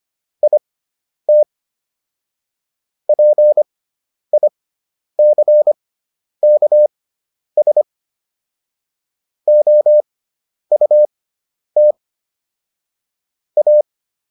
Morse code, heard as a keyed tone, about 25 wpm, spaced as for 10 wpm, 600 Hz. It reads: IT PICKS OUT A